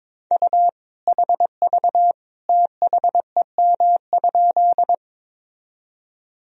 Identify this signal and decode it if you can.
Morse code, heard as a keyed tone, about 22 wpm, 700 Hz. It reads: U HV THEM?
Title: Sentences from Top 100 Words and Abbreviations 22wpm